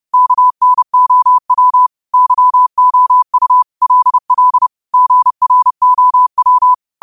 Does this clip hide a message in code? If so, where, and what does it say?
Yes: Morse code from 0.1 to 6.7 s, heard as a keyed tone, about 30 wpm, 1000 Hz. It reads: KNOW YOU LL GROW